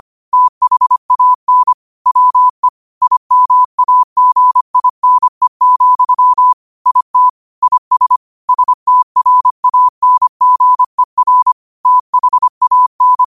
Morse code, heard as a keyed tone, about 25 wpm, 1000 Hz. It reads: THAN WE IMAGINE, IT IS STRANGER THAN
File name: qotdfiles / 25wpm